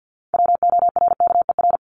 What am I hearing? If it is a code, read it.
Morse code, heard as a keyed tone, about 50 wpm, 700 Hz. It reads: WORKER
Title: Top 1751-2000 Words 50wpm